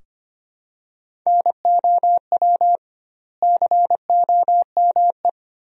Morse code, heard as a keyed tone, about 25 wpm, 700 Hz. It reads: NOW COME
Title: 2x Word-Spacing - Sentences from Top 200 Words - 25wpm